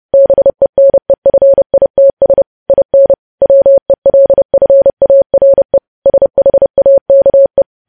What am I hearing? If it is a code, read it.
Morse code, heard as a keyed tone, about 30 wpm, 550 Hz. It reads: BENEFITS IN WELFARE SHAKE
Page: news headlines in morse code at 30 wpm